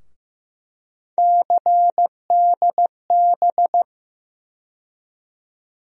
Morse code, heard as a keyed tone, about 15 wpm, 700 Hz. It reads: CDB
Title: USA Call Sign Postfixes - 3 Letters 15wpm